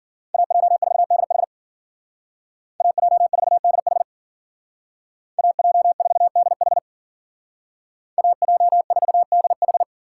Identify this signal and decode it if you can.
Morse code, heard as a keyed tone, about 50 wpm, 700 Hz. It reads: AJ4DH AJ4DH AJ4DH AJ4DH
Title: Speed Racing - USA 2x2 Call Signs 40wpm